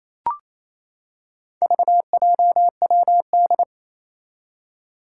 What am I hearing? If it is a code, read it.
Morse code, heard as a keyed tone, about 27 wpm, 700 Hz. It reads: VJWD
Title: Four Challenging Letter-Number Combinations 28wpm